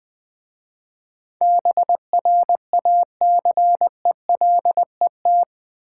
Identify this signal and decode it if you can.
Morse code, heard as a keyed tone, about 20 wpm, 700 Hz. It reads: BRACELET
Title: Easy Words with Hints - 20wpm